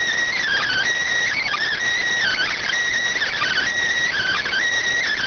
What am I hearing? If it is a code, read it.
Emergency Alert System (SAME) tones (header bursts), heard as harsh buzzing bursts. It is heard from 0.3 to 4.6 s.
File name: findings / S ss-portrait.gen-19 rx